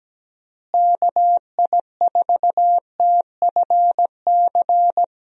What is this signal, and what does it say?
Morse code, heard as a keyed tone, about 17 wpm, 700 Hz. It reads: KI4TFC
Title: USA Call Signs - 6 Characters - Repeated 3x - 17wpm